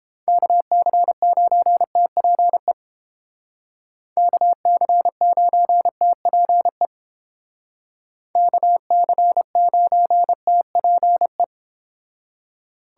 Morse code, heard as a keyed tone, about 30 wpm, 700 Hz. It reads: KC9TPE KC9TPE KC9TPE